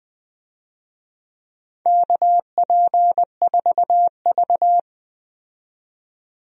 Morse code, heard as a keyed tone, about 20 wpm, 700 Hz. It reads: KP4V
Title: DXCC-100 Call Signs - 4 Characters - 20wpm